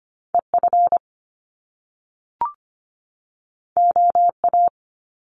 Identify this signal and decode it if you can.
Morse code, heard as a keyed tone, about 25 wpm, 700 Hz. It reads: EF OA
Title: Top Two Letter Combinations in Words 25wpm